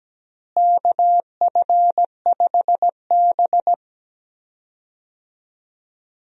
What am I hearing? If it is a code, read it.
Morse code, heard as a keyed tone, about 17 wpm, 700 Hz. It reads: KF5B
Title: USA Call Signs - 4 Characters - Repeated 2x - 17wpm